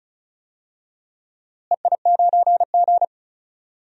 Morse code, heard as a keyed tone, about 36 wpm, 700 Hz. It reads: EI9G
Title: DXCC-100 Call Signs - 4 Characters - Repeated 2x - 35wpm